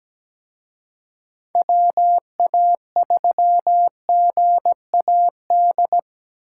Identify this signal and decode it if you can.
Morse code, heard as a keyed tone, about 20 wpm, 700 Hz. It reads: WA3GAD
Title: Speed Racing - USA 2x3 Call Signs 17wpm